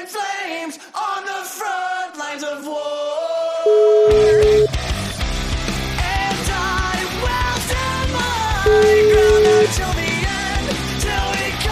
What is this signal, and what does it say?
Ringback tone (Europe, 425 Hz), with noise.